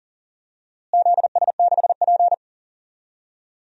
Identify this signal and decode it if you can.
Morse code, heard as a keyed tone, about 40 wpm, 700 Hz. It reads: ZS6P
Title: DXCC-100 Call Signs - 4 Characters - Repeated 3x - 40wpm